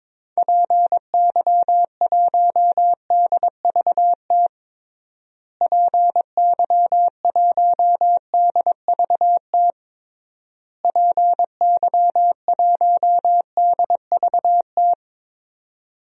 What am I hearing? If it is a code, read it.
Morse code, heard as a keyed tone, about 22 wpm, 700 Hz. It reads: PY1DVT PY1DVT PY1DVT
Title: DXCC-100 Call Signs - 6 Characters - Repeated 3x - 22wpm